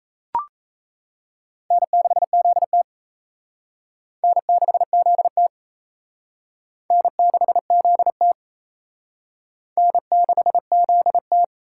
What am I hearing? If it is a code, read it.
Morse code, heard as a keyed tone, about 34 wpm, 700 Hz. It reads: N6ZT N6ZT N6ZT N6ZT
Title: Speed Racing - USA 1x2 Call Signs 28wpm